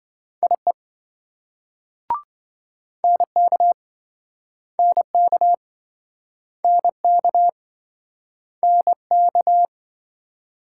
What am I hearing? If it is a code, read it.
Morse code, heard as a keyed tone, about 27 wpm, 700 Hz. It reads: IE NK NK NK NK